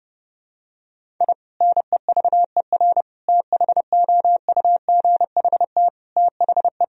Morse code, heard as a keyed tone, about 32 wpm, 700 Hz. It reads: I NEVER THOUGHT THE
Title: Sentences from Top 1000 Words - Review Part 1 - 30wpm